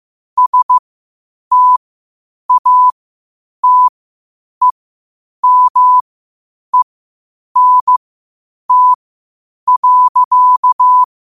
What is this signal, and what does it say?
Morse code, heard as a keyed tone, about 15 wpm, spaced as for 8 wpm, 1000 Hz. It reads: STATEMENT.
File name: qotdfiles / 8wpm